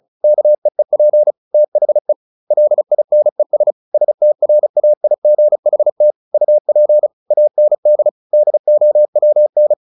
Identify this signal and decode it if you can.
Morse code, heard as a keyed tone, about 34 wpm, 600 Hz. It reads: KEEP THE LINES STRAIGHT UP AND DOWN
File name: the old man and the sea / the old man and the sea 1 35wpm